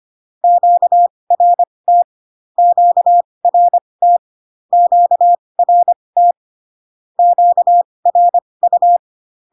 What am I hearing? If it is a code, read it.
Morse code, heard as a keyed tone, about 25 wpm, spaced as for 20 wpm, 700 Hz. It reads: QRT QRT QRT QRU
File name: cw / Q-codes most common repeated 3 times 15 25wpm ik1hge